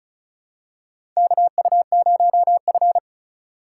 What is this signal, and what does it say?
Morse code, heard as a keyed tone, about 36 wpm, 700 Hz. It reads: KU0F
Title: USA Call Signs - 4 Characters - 3x Character Spacing - 35wpm